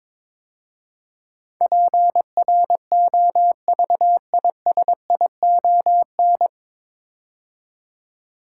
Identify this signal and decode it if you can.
Morse code, heard as a keyed tone, about 22 wpm, 700 Hz. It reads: PROVISION
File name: 9 Letter Words - First 3 Letters Given - 22wpm